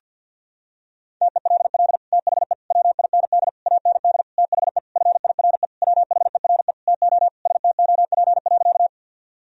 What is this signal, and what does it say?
Morse code, heard as a keyed tone, about 49 wpm, 700 Hz. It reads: TELL THE WIND AND THE FIRE WHERE TO STOP.
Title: Sentences from Top 1000 Words - Review Part 2 - 50wpm